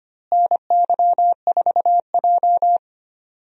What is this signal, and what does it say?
Morse code, heard as a keyed tone, about 25 wpm, 700 Hz. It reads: NY4J
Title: USA Call Signs - 4 Characters - Repeated 3x - 25wpm